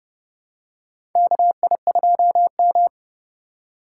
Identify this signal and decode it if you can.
Morse code, heard as a keyed tone, about 32 wpm, 700 Hz. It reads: KI2M
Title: USA Call Signs - 4 Characters - 4x Character Spacing - 30wpm